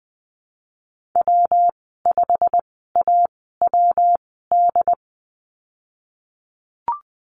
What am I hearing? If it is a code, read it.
Morse code, heard as a keyed tone, about 20 wpm, spaced as for 15 wpm, 700 Hz. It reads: W5AWD
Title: 147 Lesson 24g - Letter y - Call Signs 20wpm